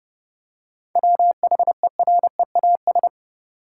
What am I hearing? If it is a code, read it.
Morse code, heard as a keyed tone, about 30 wpm, 700 Hz. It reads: WHEREAS
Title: Seven Letter Words - 30wpm